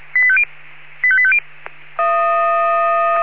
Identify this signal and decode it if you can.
Five-tone selective call (ZVEI), heard as quick runs of five tones, tones of about 70 ms. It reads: 85850
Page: SDR/ZVEI